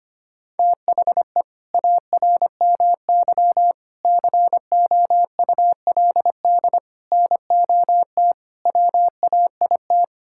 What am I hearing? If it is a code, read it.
Morse code, heard as a keyed tone, about 25 wpm, 700 Hz. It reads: THE ARMY COULD NOT WAIT
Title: Sentences from Top 1000 Words - Review Part 5 - 25wpm